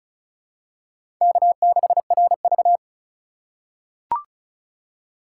Morse code, heard as a keyed tone, about 36 wpm, 700 Hz. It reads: K6RV
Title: USA Call Signs - 4 Characters - 4x Character Spacing - 35wpm